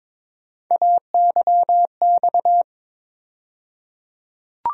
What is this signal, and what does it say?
Morse code, heard as a keyed tone, about 22 wpm, 700 Hz. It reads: AYX
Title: USA Call Sign Postfixes - 3 Letters 22wpm